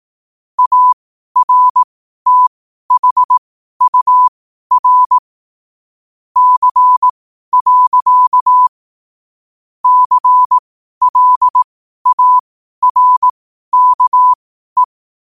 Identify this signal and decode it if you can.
Morse code, heard as a keyed tone, about 18 wpm, spaced as for 12 wpm, 1000 Hz. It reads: ARTHUR C. CLARKE
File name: qotdfiles / 12wpm